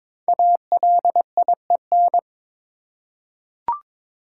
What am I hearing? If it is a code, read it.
Morse code, heard as a keyed tone, about 22 wpm, 700 Hz. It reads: ALIEN
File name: 5 Letter Words - First 3 Letters Given - 22wpm